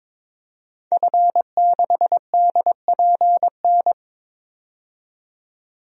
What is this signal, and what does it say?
Morse code, heard as a keyed tone, about 22 wpm, 700 Hz. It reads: F6DPN
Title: DXCC-100 Call Signs - 5 Characters - 22wpm